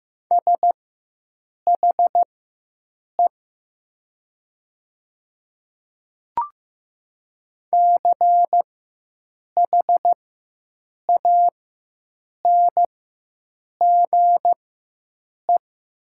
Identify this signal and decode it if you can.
Morse code, heard as a keyed tone, about 15 wpm, spaced as for 7 wpm, 700 Hz. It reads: SHE CHANGE
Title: 4x Character Spacing - Top 200 Words - 15wpm